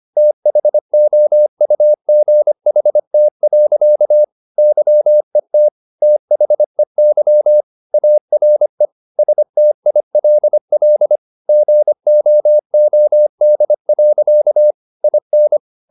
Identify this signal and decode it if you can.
Morse code, heard as a keyed tone, about 25 wpm, 600 Hz. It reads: THOUGHT. YET THEY ARE STILL GOOD. IN
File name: the old man and the sea part 1 25wpm